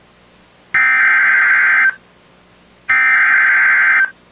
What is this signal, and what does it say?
Emergency Alert System (SAME) tones (header bursts), heard as harsh buzzing bursts. It is heard from 0.7 to 4.0 s.